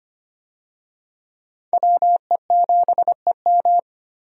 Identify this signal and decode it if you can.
Morse code, heard as a keyed tone, about 25 wpm, 700 Hz. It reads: WE7EM
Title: USA Call Signs - 5 Characters - 25wpm